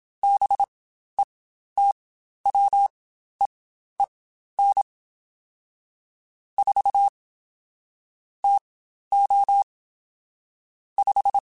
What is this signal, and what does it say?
Morse code, heard as a keyed tone, about 27 wpm, spaced as for 12 wpm, 800 Hz. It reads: BETWEEN 4 TO 5
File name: Spaced Characters / Spaced Characters-mp3s / Weather4 21 sp @25